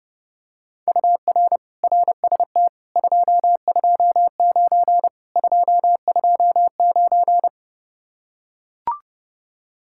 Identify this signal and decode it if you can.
Morse code, heard as a keyed tone, about 31 wpm, 700 Hz. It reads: UR RST 229 229